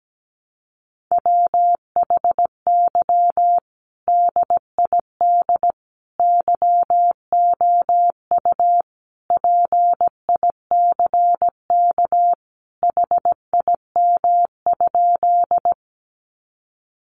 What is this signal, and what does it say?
Morse code, heard as a keyed tone, about 17 wpm, 700 Hz. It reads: WHY DID YOU PICK HIM?
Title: Sentences from Top 1000 Words 17wpm